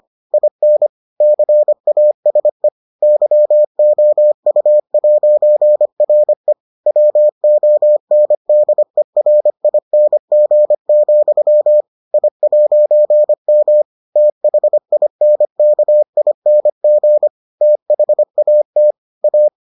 Morse code, heard as a keyed tone, about 25 wpm, 600 Hz. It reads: IN CASE YOU'RE WONDERING, I'M THINKING THAT A